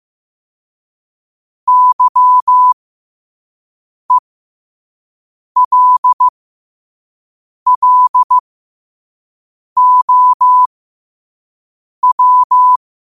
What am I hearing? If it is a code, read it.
Morse code, heard as a keyed tone, about 15 wpm, spaced as for 5 wpm, 1000 Hz. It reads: YELLOW
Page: New quotes every day in morse code at 5 Words per minute.